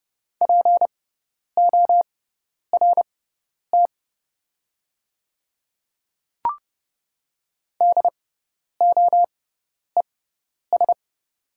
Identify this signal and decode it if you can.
Morse code, heard as a keyed tone, about 30 wpm, spaced as for 10 wpm, 700 Hz. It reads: PORT DOES
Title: 6x Character Spacing - Top 200 Words - 30wpm